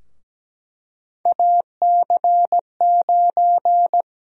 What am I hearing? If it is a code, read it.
Morse code, heard as a keyed tone, about 17 wpm, 700 Hz. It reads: AC9
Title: USA Call Sign Prefixes - Repeated 2x - 17wpm